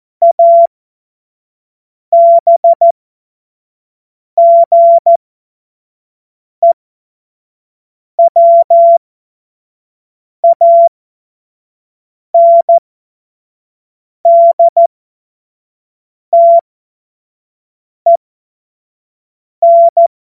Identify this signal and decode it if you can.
Morse code, heard as a keyed tone, about 14 wpm, spaced as for 5 wpm, 680 Hz. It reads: ABGEWANDTEN
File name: Morsekurs / Witze / Witz 17. Mai 2022 25 F70